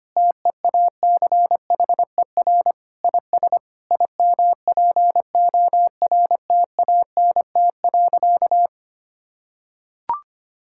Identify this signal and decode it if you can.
Morse code, heard as a keyed tone, about 25 wpm, 700 Hz. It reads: TEACHER IS IMPORTANT.
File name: practice / 25wpm / Sentences with 3-Word Phrases - 25wpm